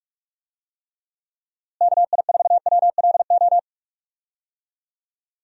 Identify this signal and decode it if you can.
Morse code, heard as a keyed tone, about 46 wpm, 700 Hz. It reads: KI4WLO
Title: USA Call Signs - 6 Characters - Repeated 2x - 45wpm